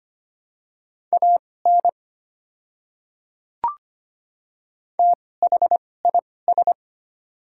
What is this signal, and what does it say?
Morse code, heard as a keyed tone, about 25 wpm, spaced as for 18 wpm, 700 Hz. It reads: AN THIS
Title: 2x Character Spacing - Top 300 Words - 25wpm